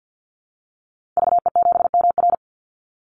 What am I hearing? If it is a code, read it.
Morse code, heard as a keyed tone, about 50 wpm, 700 Hz. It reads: VE7MR
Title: Call Signs Repeated 2x - 50wpm